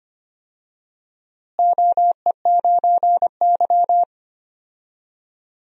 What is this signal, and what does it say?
Morse code, heard as a keyed tone, about 25 wpm, 700 Hz. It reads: OE9Y
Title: DXCC-100 Call Signs - 4 Characters - Repeated 2x - 25wpm